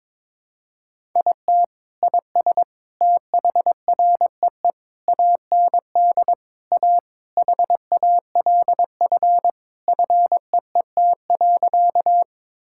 Morse code, heard as a keyed tone, about 22 wpm, 700 Hz. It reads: IT IS THREE AND A HALF FEET.
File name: practice / 22wpm / Sentences with 3-Word Phrases - 22wpm